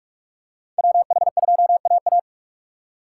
Morse code, heard as a keyed tone, about 46 wpm, 700 Hz. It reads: WH2AA
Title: USA Call Signs - 5 Characters - 2x Character Spacing - 45wpm